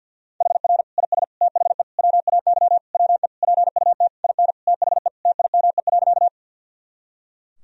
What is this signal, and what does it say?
Morse code, heard as a keyed tone, about 52 wpm, 700 Hz. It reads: HR IS THE WAY WE PUT IN THE TIME.
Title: Sentences from Top 200 Words and Abbreviations 50wpm